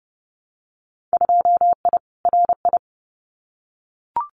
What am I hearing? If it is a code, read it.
Morse code, heard as a keyed tone, about 30 wpm, 700 Hz. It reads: 2I RI
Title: Winter Field Day Exchanges Encore 30wpm